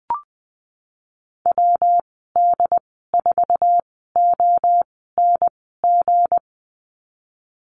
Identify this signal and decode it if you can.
Morse code, heard as a keyed tone, about 19 wpm, spaced as for 14 wpm, 700 Hz. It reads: WD4ONG
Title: 161 Lesson 26g - Letter g - Call Signs 20wpm